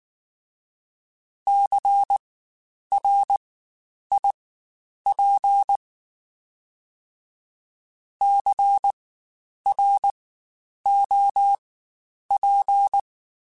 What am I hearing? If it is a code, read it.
Morse code, heard as a keyed tone, about 19 wpm, spaced as for 9 wpm, 800 Hz. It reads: CRIP CROP
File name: Spaced Alphas / Spaced Alphas mp3s / Cs sp @18